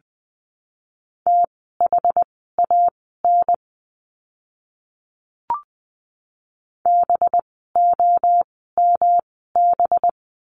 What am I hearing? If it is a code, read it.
Morse code, heard as a keyed tone, about 20 wpm, spaced as for 14 wpm, 700 Hz. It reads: THAN BOMB
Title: 185 Lesson 30f - Letter b - Words 20wpm